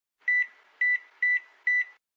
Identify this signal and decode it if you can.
There is no code.